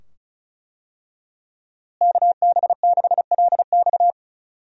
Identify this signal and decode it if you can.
Morse code, heard as a keyed tone, about 35 wpm, 700 Hz. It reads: KB6LX
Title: USA Call Signs - 5 Characters - Repeated 2x - 35wpm